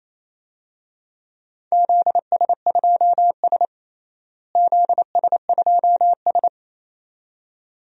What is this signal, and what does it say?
Morse code, heard as a keyed tone, about 27 wpm, 700 Hz. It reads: ZS2S ZS2S
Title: DXCC-100 Call Signs - 4 Characters - Repeated 2x - 28wpm